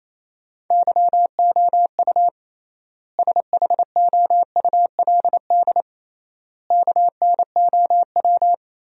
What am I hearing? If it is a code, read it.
Morse code, heard as a keyed tone, about 28 wpm, 700 Hz. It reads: YOU SHOULD KNOW
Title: 3x Word-Spacing - Sentences from Top 400 Words - 28wpm